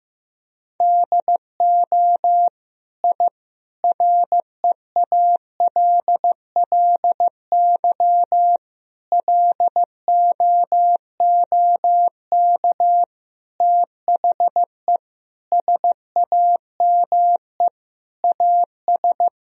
Morse code, heard as a keyed tone, about 15 wpm, 700 Hz. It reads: DO I REALLY LOOK THE SAME AS
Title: Sentences from Top 1000 Words - Review Part 2 - 15wpm